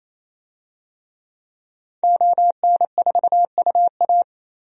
Morse code, heard as a keyed tone, about 28 wpm, 700 Hz. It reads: ON4UA
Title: DXCC-100 Call Signs - 5 Characters - 28wpm